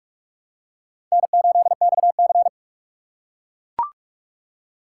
Morse code, heard as a keyed tone, about 44 wpm, 700 Hz. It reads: N8XC
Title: USA Call Signs - 4 Characters - 6x Character Spacing - 45wpm